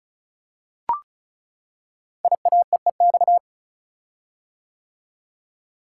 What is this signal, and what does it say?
Morse code, heard as a keyed tone, about 36 wpm, 690 Hz. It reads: IAEEX
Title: Code Groups - 5 Letter:Numbers - 35wpm